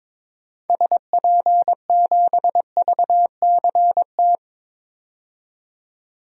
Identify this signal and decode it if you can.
Morse code, heard as a keyed tone, about 22 wpm, 700 Hz. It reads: SP7VCT
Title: DXCC-100 Call Signs - 6 Characters - 22wpm